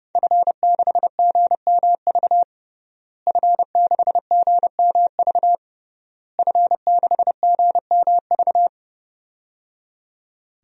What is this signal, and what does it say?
Morse code, heard as a keyed tone, about 30 wpm, 700 Hz. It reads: F6GMV F6GMV F6GMV
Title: DXCC-100 Call Signs - 5 Characters - Repeated 3x - 30wpm